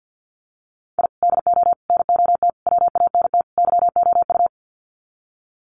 Morse code, heard as a keyed tone, about 50 wpm, 700 Hz. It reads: I DO NOT WANT YOU
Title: Sets of 5 Words 50wpm